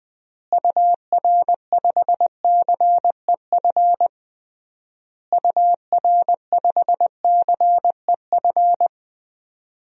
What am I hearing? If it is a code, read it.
Morse code, heard as a keyed tone, about 20 wpm, 700 Hz. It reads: UR5CEF UR5CEF
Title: DXCC-100 Call Signs - 6 Characters - Repeated 3x - 20wpm